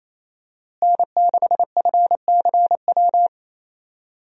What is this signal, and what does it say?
Morse code, heard as a keyed tone, about 28 wpm, 700 Hz. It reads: N6FCW
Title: USA Call Signs - 5 Characters - Repeated 3x - 28wpm